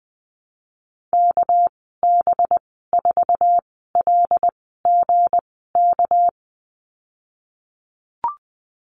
Morse code, heard as a keyed tone, about 20 wpm, spaced as for 15 wpm, 700 Hz. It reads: KB4LGK
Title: 238 Lesson 38g - Letter q - Call Signs 20wpm